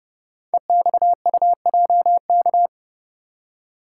Morse code, heard as a keyed tone, about 30 wpm, 690 Hz. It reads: EXUJK
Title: Code Groups - 5 Letters - 30wpm